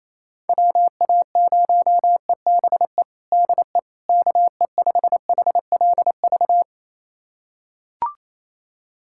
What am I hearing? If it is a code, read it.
Morse code, heard as a keyed tone, about 28 wpm, 700 Hz. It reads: WA0EBE DE KE5HLV
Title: USA Callsign DE Callsign - 6 Characters - 28wpm